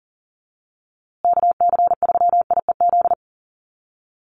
Morse code, heard as a keyed tone, about 40 wpm, 700 Hz. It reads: KC3IEZ
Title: USA Call Signs Repeated 3x 40wpm